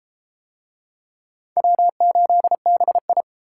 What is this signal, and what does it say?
Morse code, heard as a keyed tone, about 33 wpm, 700 Hz. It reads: W8BI